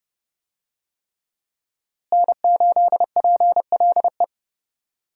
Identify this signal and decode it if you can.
Morse code, heard as a keyed tone, about 30 wpm, 700 Hz. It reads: N8PLE
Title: USA Call Signs - 5 Characters - Repeated 3x - 30wpm